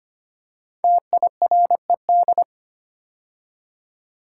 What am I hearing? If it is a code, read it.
Morse code, heard as a keyed tone, about 25 wpm, 700 Hz. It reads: TIRED
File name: 5 Letter Words - First 3 Letters Given - 25wpm